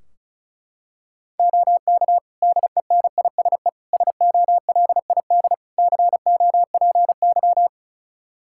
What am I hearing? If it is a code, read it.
Morse code, heard as a keyed tone, about 36 wpm, 700 Hz. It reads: OK DENISE SOLID COPY
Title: Practice QSO - Exchange 3 - More Info - 35wpm